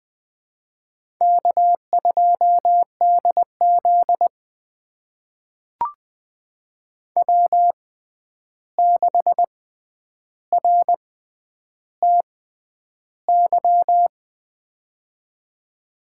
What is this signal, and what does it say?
Morse code, heard as a keyed tone, about 20 wpm, spaced as for 10 wpm, 700 Hz. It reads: K2DZ W6RTY
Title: USA Call Signs - 6x Character Spacing - 20wpm